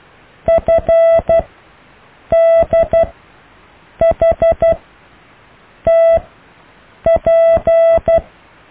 Morse code, heard as a keyed tone, about 12 wpm, spaced as for 6 wpm, 660 Hz. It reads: FDHTP
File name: cw / EKTPV+BFHNW